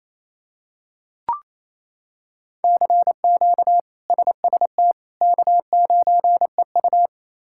Morse code, heard as a keyed tone, about 28 wpm, 700 Hz. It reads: CQ SST K9EU
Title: K1USN SST QSO Practice - 28wpm